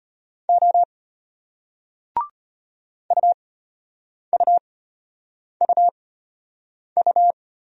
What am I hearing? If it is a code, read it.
Morse code, heard as a keyed tone, about 34 wpm, spaced as for 8 wpm, 700 Hz. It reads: O UUUU